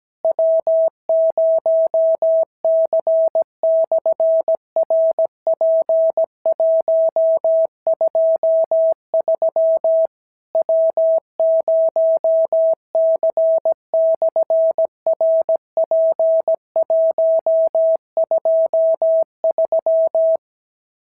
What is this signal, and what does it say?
Morse code, heard as a keyed tone, about 17 wpm, 650 Hz. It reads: W0C/RP123 W0C/RP123
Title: SOTA QSO Practice 17wpm